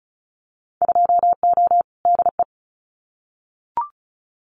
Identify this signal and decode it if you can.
Morse code, heard as a keyed tone, about 34 wpm, 700 Hz. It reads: 2O DE
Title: Winter Field Day Exchanges Encore 35wpm